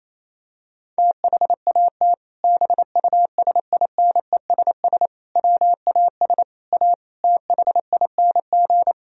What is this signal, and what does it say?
Morse code, heard as a keyed tone, about 28 wpm, 700 Hz. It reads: THAT BUSINESS WAS A THING
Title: Sentences from Top 1000 Words - Review Part 3 - 28wpm